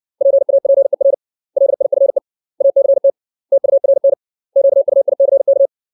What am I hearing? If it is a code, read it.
Morse code, heard as a keyed tone, about 60 wpm, 550 Hz. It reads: PAPER LIFE ACT TURN QUICK
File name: 632 words / 632 Words 60WPM